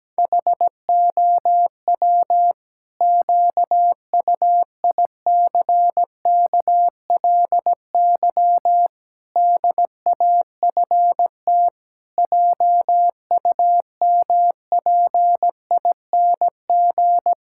Morse code, heard as a keyed tone, about 17 wpm, 700 Hz. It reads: HOW QUICKLY DAFT JUMPING
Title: Pangram Challenge v2 - 17wpm